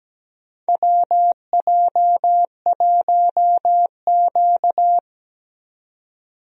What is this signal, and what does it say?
Morse code, heard as a keyed tone, about 17 wpm, 700 Hz. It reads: WJ1Q
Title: USA Call Signs - 4 Characters - 17wpm